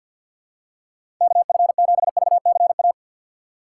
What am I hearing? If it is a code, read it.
Morse code, heard as a keyed tone, about 53 wpm, 700 Hz. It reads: KF7VCA
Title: USA Call Signs - 6 Characters - 8x Character Spacing - 50wpm